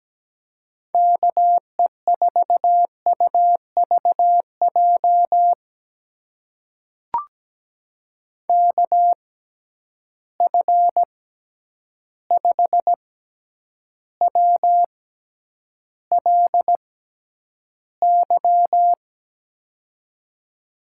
Morse code, heard as a keyed tone, about 17 wpm, spaced as for 9 wpm, 700 Hz. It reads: KE4UVJ KF5WLY
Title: USA Call Signs - 6 Characters - 6x Character Spacing - 17wpm